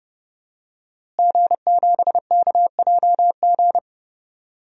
Morse code, heard as a keyed tone, about 30 wpm, 700 Hz. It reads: G7KJG
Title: DXCC-100 Call Signs - 5 Characters - Repeated 3x - 30wpm